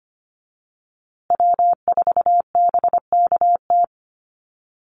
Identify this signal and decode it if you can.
Morse code, heard as a keyed tone, about 25 wpm, 700 Hz. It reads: W4BKT
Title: USA Call Signs Repeated 2x 25wpm